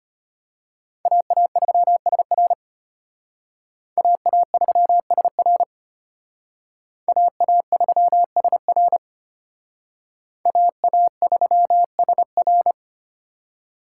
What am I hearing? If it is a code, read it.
Morse code, heard as a keyed tone, about 32 wpm, 700 Hz. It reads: AA3SR AA3SR AA3SR AA3SR